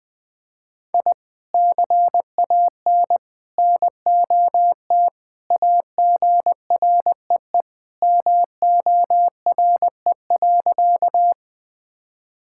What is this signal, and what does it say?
Morse code, heard as a keyed tone, about 20 wpm, 700 Hz. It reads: I CAN NOT AGREE MORE.
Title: Sentences with 3-Word Phrases - Spoken Beginning - 20wpm